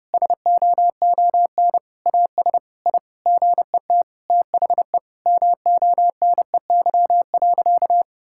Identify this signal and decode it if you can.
Morse code, heard as a keyed tone, about 30 wpm, 700 Hz. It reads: SOON AS I GET THE MONEY.
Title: Sentences from Top 1000 Words - Review Part 5 - 30wpm